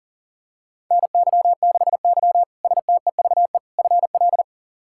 Morse code, heard as a keyed tone, about 40 wpm, 700 Hz. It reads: NY6Y STEVE FL
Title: K1USN SST QSO Practice - 40wpm